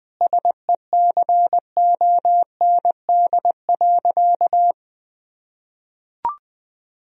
Morse code, heard as a keyed tone, about 20 wpm, 700 Hz. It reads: SECOND.
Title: Sentences from Top 1000 Words - Review Part 1 - 20wpm